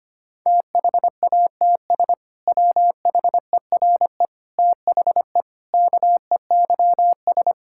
Morse code, heard as a keyed tone, about 25 wpm, 700 Hz. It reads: THATS WHERE THE KEYS
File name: practice / 25wpm / Sentences with 2-Word Phrases - 25wpm